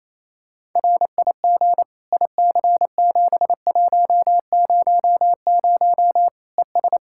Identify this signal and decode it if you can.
Morse code, heard as a keyed tone, about 28 wpm, 700 Hz. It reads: RIG IC7100 ES
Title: Practice QSO - Exchange 2 - Info - 28wpm